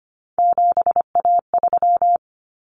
Morse code, heard as a keyed tone, about 25 wpm, 700 Hz. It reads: 7A3
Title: Common Cut Numbers - 3-Digit 25wpm